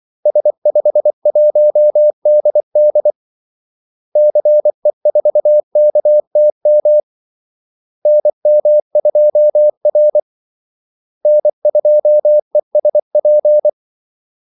Morse code, heard as a keyed tone, about 24 wpm, 600 Hz. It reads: S51DD CE4KTM NM2R N2ESP
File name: Callsigns-24wpm